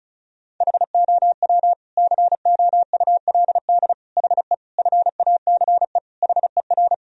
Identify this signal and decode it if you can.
Morse code, heard as a keyed tone, about 36 wpm, 700 Hz. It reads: HOW COULD HE FACE HER
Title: Sentences from Top 1000 Words - Review Part 2 - 35wpm